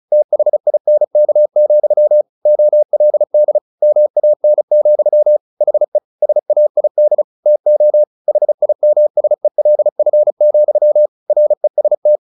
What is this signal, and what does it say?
Morse code, heard as a keyed tone, about 34 wpm, 600 Hz. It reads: THINK, OLD MAN, HE SAID TO HIMSELF, REST
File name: the old man and the sea / the old man and the sea 1 35wpm